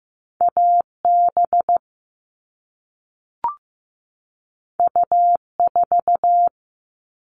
Morse code, heard as a keyed tone, about 15 wpm, 700 Hz. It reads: AB U4
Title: All Cut Numbers - 2-Digit 15wpm